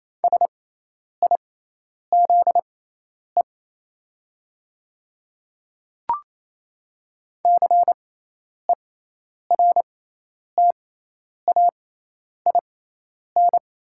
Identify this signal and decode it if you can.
Morse code, heard as a keyed tone, about 28 wpm, spaced as for 10 wpm, 700 Hz. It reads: SIZE CERTAIN
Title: 6x Character Spacing - Top 500 Words - 28wpm